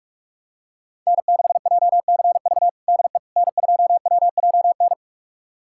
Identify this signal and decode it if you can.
Morse code, heard as a keyed tone, about 46 wpm, 700 Hz. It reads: N6JCV DE N2WJN
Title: USA Callsign DE Callsign - 5 Characters - 45wpm